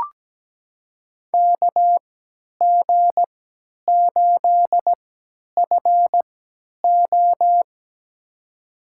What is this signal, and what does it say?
Morse code, heard as a keyed tone, about 17 wpm, spaced as for 10 wpm, 700 Hz. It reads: KG8FO
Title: USA Call Signs - 5 Characters - 3x Character Spacing - 17wpm